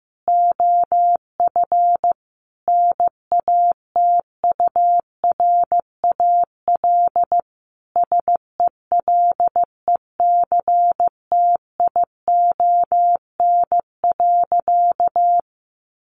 Morse code, heard as a keyed tone, about 15 wpm, 700 Hz. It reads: OF NATURAL SELECTION.
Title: Sentences from Top 1400 Words 15wpm